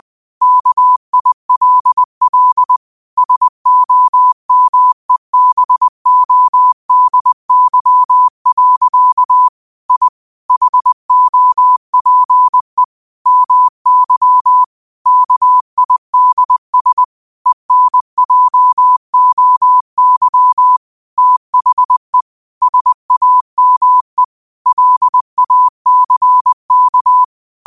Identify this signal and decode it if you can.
Morse code, heard as a keyed tone, about 20 wpm, 1000 Hz. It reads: KILL SOMEBODY. I HOPE MY KIDS ENJOY THE SAME LACK
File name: qotdfiles / 20wpm